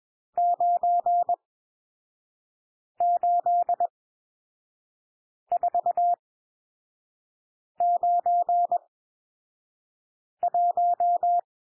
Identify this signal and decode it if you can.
Morse code, heard as a keyed tone, about 21 wpm, spaced as for 5 wpm, 700 Hz. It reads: 98491
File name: cwmp3 / N1 SOUND NUMBERS 21 wpm